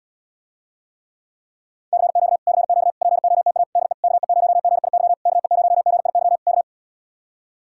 Morse code, heard as a keyed tone, about 77 wpm, 700 Hz. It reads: CQ CQ POTA DE KE0GIY KE0GIY K